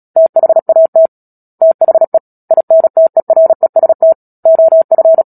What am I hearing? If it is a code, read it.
Morse code, heard as a keyed tone, about 37 wpm, 650 Hz. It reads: THAT THE INTEREST OF
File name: Book Two / 10-Chapter 10 - Epilogue - 36 WPM